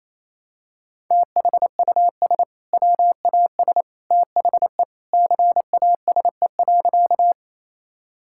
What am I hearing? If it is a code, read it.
Morse code, heard as a keyed tone, about 28 wpm, 700 Hz. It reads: THUS WAS THE CASE.
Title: Sentences from Top 1000 Words - Review Part 5 - 28wpm